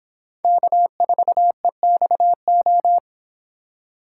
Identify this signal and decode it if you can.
Morse code, heard as a keyed tone, about 26 wpm, 700 Hz. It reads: K4EXO
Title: Speed Racing - USA Call Signs 22wpm